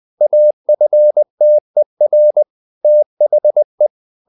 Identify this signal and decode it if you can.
Morse code, heard as a keyed tone, about 20 wpm, 600 Hz. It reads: AFTER THE
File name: the old man and the sea part 2 20wpm